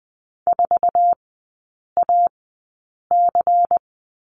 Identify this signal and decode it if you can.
Morse code, heard as a keyed tone, about 20 wpm, spaced as for 8 wpm, 690 Hz. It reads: 4AC
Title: 159 Lesson 26e - Letter g - Three Characters 20wpm